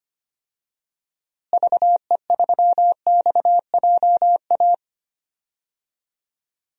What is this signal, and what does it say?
Morse code, heard as a keyed tone, about 25 wpm, 700 Hz. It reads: VE3XJA
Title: DXCC-100 Call Signs - 6 Characters - Repeated 2x - 25wpm